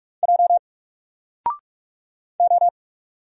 Morse code, heard as a keyed tone, about 44 wpm, 700 Hz. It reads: J O
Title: Single Letter-Number Alt 45wpm